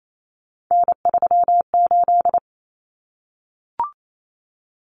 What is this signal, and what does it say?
Morse code, heard as a keyed tone, about 28 wpm, 700 Hz. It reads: N38
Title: Common Cut Numbers - 3-Digit 28wpm